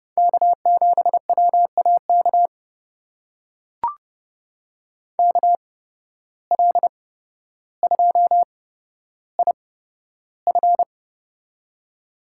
Morse code, heard as a keyed tone, about 31 wpm, spaced as for 13 wpm, 700 Hz. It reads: K7WAK KL2IF
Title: USA Call Signs - 5 Characters - 8x Character Spacing - 30wpm